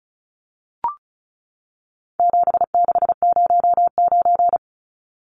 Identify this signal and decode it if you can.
Morse code, heard as a keyed tone, about 34 wpm, 700 Hz. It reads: 7609
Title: Numbers - 4-Digit 35wpm